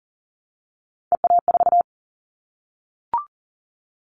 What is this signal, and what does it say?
Morse code, heard as a keyed tone, about 40 wpm, 700 Hz. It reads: EA4
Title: Callsign Prefixes for Top 25 DXCC - 40wpm